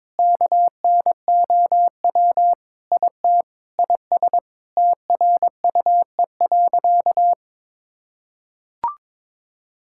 Morse code, heard as a keyed tone, about 22 wpm, 700 Hz. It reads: KNOW IT IS TRUE.
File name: practice / 22wpm / Sentences with 4-Word Phrases - 22wpm